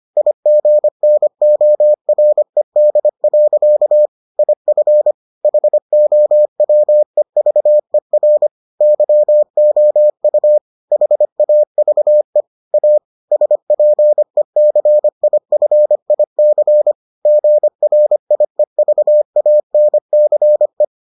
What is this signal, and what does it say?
Morse code, heard as a keyed tone, about 25 wpm, 600 Hz. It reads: IGNORED. IF HOWEVER YOU HAVE A SPECIFIC GRIEVANCE